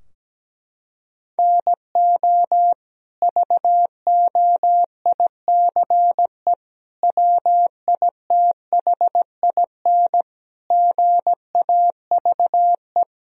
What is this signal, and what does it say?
Morse code, heard as a keyed tone, about 17 wpm, 700 Hz. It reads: NO VOICE WITHIN GAVE
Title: Sentences from Top 1000 Words - Review Part 4 - 17wpm